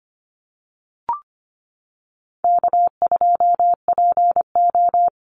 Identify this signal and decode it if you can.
Morse code, heard as a keyed tone, about 25 wpm, 700 Hz. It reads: K2PO
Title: Call Signs - 25wpm